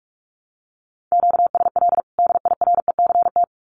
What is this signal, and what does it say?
Morse code, heard as a keyed tone, about 44 wpm, 700 Hz. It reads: QSL DIRECT
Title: Common QSO Phrases 45wpm